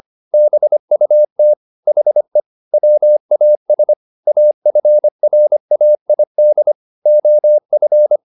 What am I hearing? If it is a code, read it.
Morse code, heard as a keyed tone, about 25 wpm, 600 Hz. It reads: BUT HE WAS AFRAID OF
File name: the old man and the sea part 1 25wpm